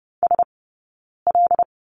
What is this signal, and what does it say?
Morse code, heard as a keyed tone, about 30 wpm, spaced as for 9 wpm, 690 Hz. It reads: SL
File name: Lesson 17d - Letter u - Two Characters 30wpm